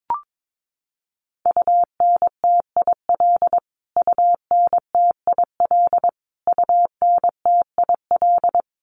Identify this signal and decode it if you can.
Morse code, heard as a keyed tone, about 22 wpm, 700 Hz. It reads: UNTIL UNTIL UNTIL
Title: Top 500 Words Repeated 22wpm